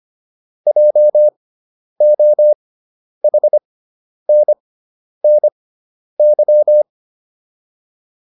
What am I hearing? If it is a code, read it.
Morse code, heard as a keyed tone, about 25 wpm, spaced as for 10 wpm, 600 Hz. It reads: JOHNNY
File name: QSO306 10